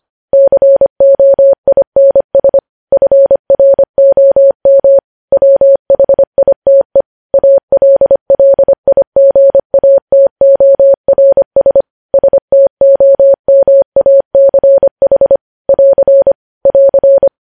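Morse code, heard as a keyed tone, about 25 wpm, 550 Hz. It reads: COINS FROM WHITE ALLIGATORS STOMACH <AR> <AR>
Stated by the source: Morse code news headlines at 25 WPM.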